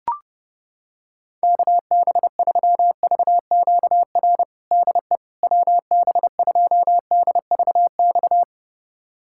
Morse code, heard as a keyed tone, about 30 wpm, 700 Hz. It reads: KB3VQR DE WB2DVX
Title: USA Callsign DE Callsign - 6 Characters - 30wpm